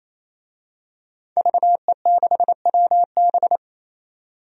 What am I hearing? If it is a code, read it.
Morse code, heard as a keyed tone, about 28 wpm, 700 Hz. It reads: VE6WB
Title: DXCC-100 Call Signs - 5 Characters - 28wpm